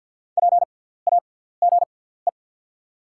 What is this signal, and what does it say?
Morse code, heard as a keyed tone, about 48 wpm, spaced as for 17 wpm, 690 Hz. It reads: PAGE
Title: 6x Character Spacing - Top 300 Words - 50wpm